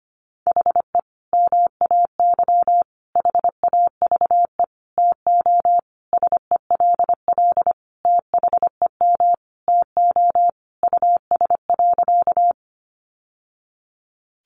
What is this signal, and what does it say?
Morse code, heard as a keyed tone, about 25 wpm, 700 Hz. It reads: HE MAY HAVE TO SELL THEM TO US.
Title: Sentences from Top 1100 Words 25wpm